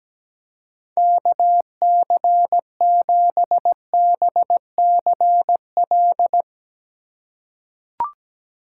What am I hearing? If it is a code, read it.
Morse code, heard as a keyed tone, about 17 wpm, 700 Hz. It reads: KC7BCL
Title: USA Call Signs - 6 Characters - 4x Character Spacing - 17wpm